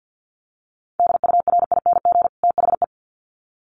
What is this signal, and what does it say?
Morse code, heard as a keyed tone, about 50 wpm, 700 Hz. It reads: DURING THE
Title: Sets of 2 Words 50wpm